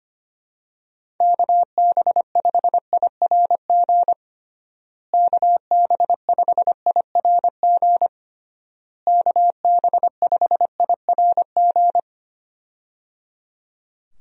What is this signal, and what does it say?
Morse code, heard as a keyed tone, about 25 wpm, 700 Hz. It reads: KB5IRG KB5IRG KB5IRG
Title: USA Call Signs - 6 Characters - Repeated 3x - 25wpm